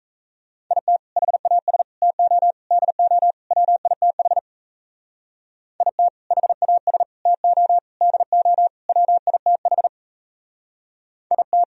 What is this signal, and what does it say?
Morse code, heard as a keyed tone, about 35 wpm, 700 Hz. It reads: IT HAS TO DO WITH IT HAS TO DO WITH IT